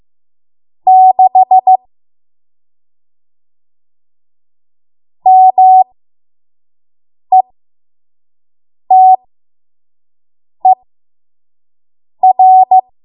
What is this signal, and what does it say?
Morse code, heard as a keyed tone, about 15 wpm, spaced as for 5 wpm, 750 Hz. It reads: 6 METER
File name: Morse / 05WPM